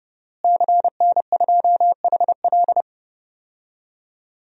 Morse code, heard as a keyed tone, about 30 wpm, 700 Hz. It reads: CN2HL
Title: DXCC-100 Call Signs - 5 Characters - Repeated 3x - 30wpm